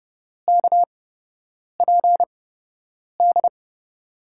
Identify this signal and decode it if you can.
Morse code, heard as a keyed tone, about 31 wpm, spaced as for 8 wpm, 700 Hz. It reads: KPD
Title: USA Call Signs - 6 Characters - 8x Character Spacing - 30wpm